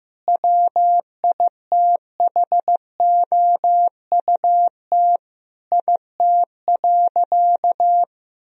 Morse code, heard as a keyed tone, about 15 wpm, 700 Hz. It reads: WITHOUT IT.
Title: Sentences from Top 1000 Words - Review Part 3 - 15wpm